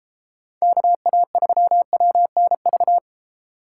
Morse code, heard as a keyed tone, about 33 wpm, 700 Hz. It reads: KA3WNV